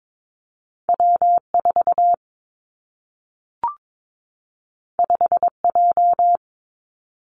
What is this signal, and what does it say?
Morse code, heard as a keyed tone, about 22 wpm, 700 Hz. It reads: W 4 5 J
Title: Two Challenging Letter-Number Combinations 22wpm